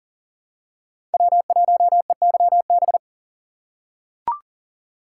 Morse code, heard as a keyed tone, about 40 wpm, 700 Hz. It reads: W1EYB
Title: USA Call Signs - 6x Character Spacing - 40wpm